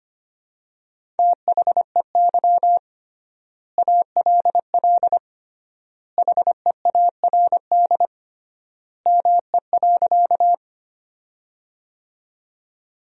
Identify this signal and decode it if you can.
Morse code, heard as a keyed tone, about 25 wpm, 700 Hz. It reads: THEY ALL HEARD ME.
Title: 3x Word-Spacing - Sentences from Top 400 Words - 25wpm